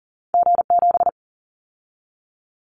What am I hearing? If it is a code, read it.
Morse code, heard as a keyed tone, about 40 wpm, 700 Hz. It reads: G7
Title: Callsign Prefixes for Top 25 DXCC - 40wpm